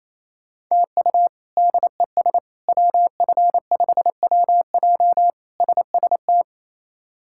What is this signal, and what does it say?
Morse code, heard as a keyed tone, about 28 wpm, 700 Hz. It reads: TU DES WF5WJ SST
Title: K1USN SST QSO Practice - 28wpm